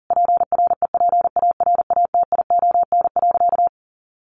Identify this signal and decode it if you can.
Morse code, heard as a keyed tone, about 40 wpm, 700 Hz. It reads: PREPARATION.
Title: Sentences from Top 1400 Words 40wpm